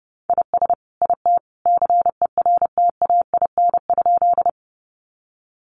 Morse code, heard as a keyed tone, about 28 wpm, 700 Hz. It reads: IS IT CERTAIN?
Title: Sentences from Top 500 Words 30wpm